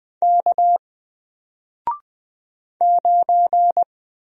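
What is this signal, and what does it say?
Morse code, heard as a keyed tone, about 20 wpm, 700 Hz. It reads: K 9
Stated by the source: Single letter and number Morse drill at 20 wpm, from N4LCD.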